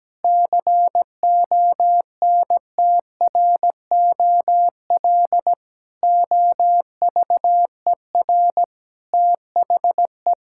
Morse code, heard as a keyed tone, about 17 wpm, 700 Hz. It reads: CONTROL OVER THE
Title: Sentences from Top 1000 Words - Review Part 5 - 17wpm